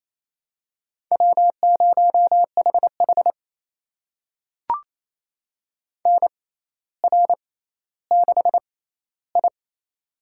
Morse code, heard as a keyed tone, about 27 wpm, spaced as for 14 wpm, 700 Hz. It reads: W0HH NR6I
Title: USA Call Signs - 4 Characters - 6x Character Spacing - 28wpm